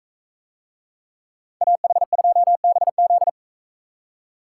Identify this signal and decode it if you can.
Morse code, heard as a keyed tone, about 44 wpm, 700 Hz. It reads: AH2BZ